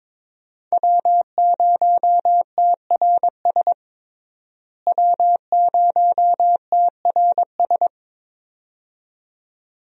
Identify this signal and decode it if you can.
Morse code, heard as a keyed tone, about 22 wpm, 700 Hz. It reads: W0TRS W0TRS
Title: USA Call Signs - 5 Characters - Repeated 2x - 22wpm